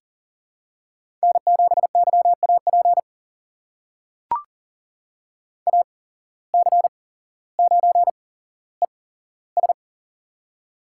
Morse code, heard as a keyed tone, about 40 wpm, spaced as for 17 wpm, 700 Hz. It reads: N7YAP AC9ES
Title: USA Call Signs - 5 Characters - 8x Character Spacing - 40wpm